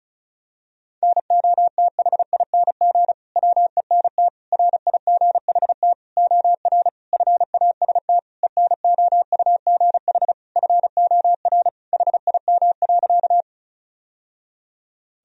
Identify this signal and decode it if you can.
Morse code, heard as a keyed tone, about 36 wpm, 700 Hz. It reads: NOTHING WENT RIGHT OR FAST ENOUGH FOR HIM.
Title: Sentences from Top 1000 Words - Review Part 2 - 35wpm